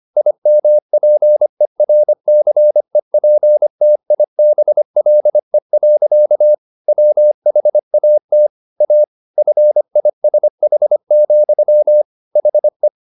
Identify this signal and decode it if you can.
Morse code, heard as a keyed tone, about 25 wpm, 600 Hz. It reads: IMPERCEPTIBLE. WHAT A FISH, HE
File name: the old man and the sea part 1 25wpm